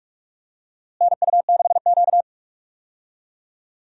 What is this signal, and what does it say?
Morse code, heard as a keyed tone, about 45 wpm, 700 Hz. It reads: NU6Q
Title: USA Call Signs - 4 Characters - Repeated 3x - 45wpm